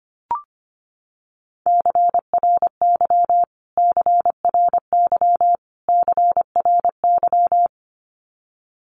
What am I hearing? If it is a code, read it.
Morse code, heard as a keyed tone, about 24 wpm, 700 Hz. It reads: CRY CRY CRY
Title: Top 500 Words Repeated 25wpm